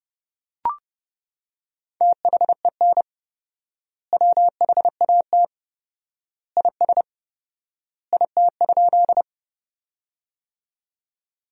Morse code, heard as a keyed tone, about 32 wpm, 700 Hz. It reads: THEN WHAT IS IT?
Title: 4x Word-Spacing - Sentences from Top 100 Words - 30wpm